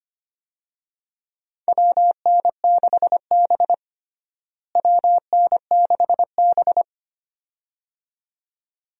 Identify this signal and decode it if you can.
Morse code, heard as a keyed tone, about 25 wpm, 700 Hz. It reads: WN6B WN6B
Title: USA Call Signs - 4 Characters - Repeated 2x - 25wpm